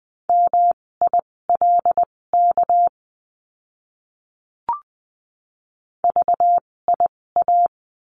Morse code, heard as a keyed tone, about 20 wpm, spaced as for 16 wpm, 700 Hz. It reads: MILK VIA
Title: 256 Lesson 42a - Words 12-20wpm